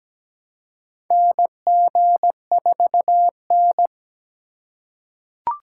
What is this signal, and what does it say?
Morse code, heard as a keyed tone, about 17 wpm, 700 Hz. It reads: NG4N
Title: USA Call Signs - 3x Character Spacing - 17wpm